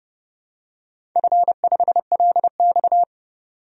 Morse code, heard as a keyed tone, about 30 wpm, 700 Hz. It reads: F5LX
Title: DXCC-100 Call Signs - 4 Characters - 30wpm